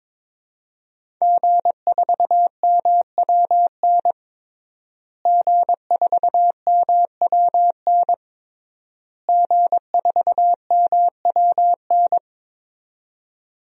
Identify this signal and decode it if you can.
Morse code, heard as a keyed tone, about 22 wpm, 700 Hz. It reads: G4MWN G4MWN G4MWN
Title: DXCC-100 Call Signs - 5 Characters - Repeated 3x - 22wpm